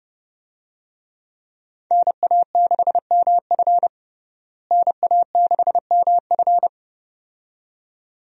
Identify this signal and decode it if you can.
Morse code, heard as a keyed tone, about 32 wpm, 700 Hz. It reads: NA6MF NA6MF
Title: USA Call Signs - 5 Characters - Repeated 2x - 30wpm